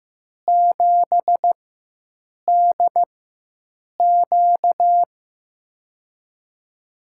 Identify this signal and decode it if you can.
Morse code, heard as a keyed tone, about 15 wpm, 700 Hz. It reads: 7 D Q
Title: USA Call Signs - 4 Characters - 4x Character Spacing - 15wpm